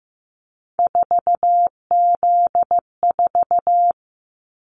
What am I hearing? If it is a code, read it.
Morse code, heard as a keyed tone, about 15 wpm, 700 Hz. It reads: 4Z4
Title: Callsign Prefixes for Top 50 DXCC - 15wpm